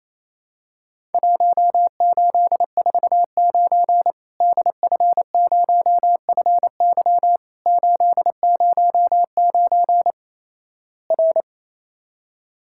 Morse code, heard as a keyed tone, about 28 wpm, 700 Hz. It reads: 1849 DF0FY 809 R
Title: Worked All Europe DX Contest - QTC Exchange - 28wpm